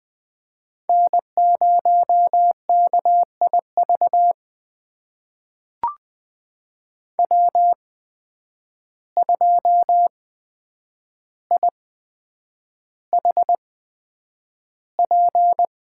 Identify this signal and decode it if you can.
Morse code, heard as a keyed tone, about 20 wpm, spaced as for 9 wpm, 700 Hz. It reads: N0KIV W2IHP
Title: USA Call Signs - 5 Characters - 8x Character Spacing - 20wpm